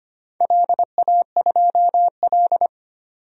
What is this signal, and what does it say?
Morse code, heard as a keyed tone, about 25 wpm, 700 Hz. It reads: LA2L
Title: DXCC-100 Call Signs - 4 Characters - 25wpm